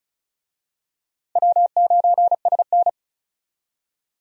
Morse code, heard as a keyed tone, about 36 wpm, 700 Hz. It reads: W9SN
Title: USA Call Signs - 4 Characters - 35wpm